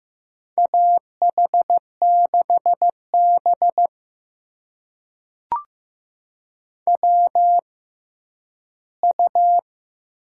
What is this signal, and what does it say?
Morse code, heard as a keyed tone, about 15 wpm, spaced as for 10 wpm, 700 Hz. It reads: AH6B WU
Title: USA Call Signs - 4 Characters - 6x Character Spacing - 15wpm